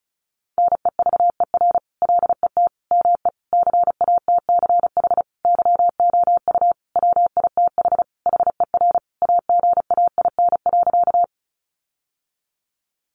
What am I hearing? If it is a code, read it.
Morse code, heard as a keyed tone, about 34 wpm, 700 Hz. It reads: NEVER LET ME CATCH YOU WITH HER AGAIN.
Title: Sentences from Top 1500 Words 35wpm